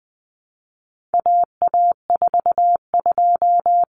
Morse code, heard as a keyed tone, about 20 wpm, 700 Hz. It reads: AA42
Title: Common Cut Numbers - 4-Digit 20wpm